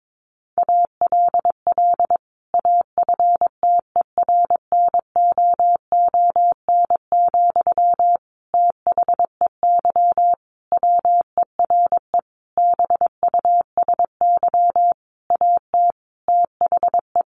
Morse code, heard as a keyed tone, about 22 wpm, 700 Hz. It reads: ALL AFTERNOON, THEY WERE BUSY AT THE
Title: Sentences from Top 1500 Words 22wpm